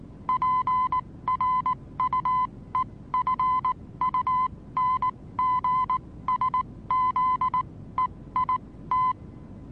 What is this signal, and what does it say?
Morse code, heard as a keyed tone, about 19 wpm, spaced as for 15 wpm, 1020 Hz. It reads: PRUEFUNGSZEIT
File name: repos / Morsekurs / DJ4UF Morsen26